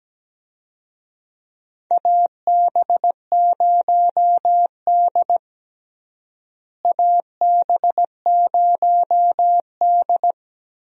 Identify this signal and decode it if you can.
Morse code, heard as a keyed tone, about 17 wpm, 700 Hz. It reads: AB0D AB0D
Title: USA Call Signs - 4 Characters - Repeated 2x - 17wpm